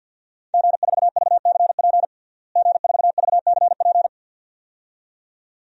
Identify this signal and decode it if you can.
Morse code, heard as a keyed tone, about 46 wpm, 700 Hz. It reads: G4VCP G4VCP
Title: DXCC-100 Call Signs - 5 Characters - Repeated 2x - 50wpm